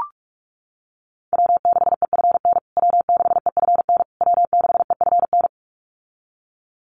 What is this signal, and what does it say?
Morse code, heard as a keyed tone, about 44 wpm, 700 Hz. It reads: W6EFN W6EFN W6EFN
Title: USA Call Signs Repeated 3x 45wpm